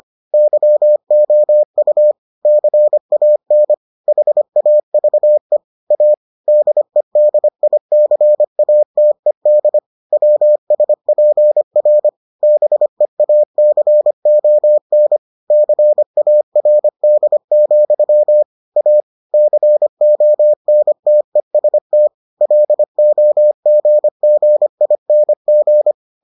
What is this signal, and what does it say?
Morse code, heard as a keyed tone, about 25 wpm, 600 Hz. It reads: YOU CAN HAVE A DEDICATED WSPR BEACON CARD, A CONTEST LOGGING